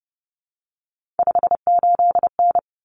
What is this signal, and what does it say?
Morse code, heard as a keyed tone, about 30 wpm, 700 Hz. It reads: 58N
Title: Common Cut Numbers - 3-Digit 30wpm